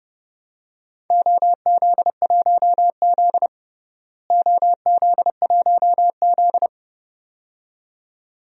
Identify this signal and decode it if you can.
Morse code, heard as a keyed tone, about 30 wpm, 700 Hz. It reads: OZ1Z OZ1Z
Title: DXCC-100 Call Signs - 4 Characters - Repeated 2x - 30wpm